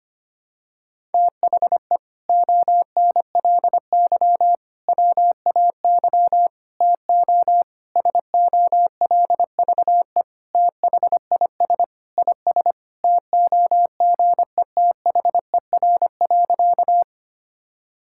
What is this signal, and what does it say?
Morse code, heard as a keyed tone, about 25 wpm, 700 Hz. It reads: THE ONLY WAY TO SOLVE THIS IS TOGETHER.
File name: practice / 25wpm / Sentences with 4-Word Phrases - 25wpm